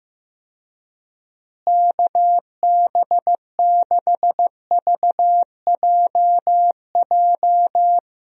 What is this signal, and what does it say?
Morse code, heard as a keyed tone, about 15 wpm, 700 Hz. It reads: KB6VJJ
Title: USA Call Signs - 6 Characters - Repeated 3x - 15wpm